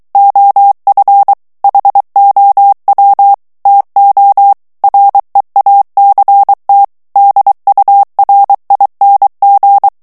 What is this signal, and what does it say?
Morse code, heard as a keyed tone, about 23 wpm, 800 Hz. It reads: OF HOW TO REACT DURING
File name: morse mp3 / ANGLAIS texte clair 2 vitesse 22 mots minute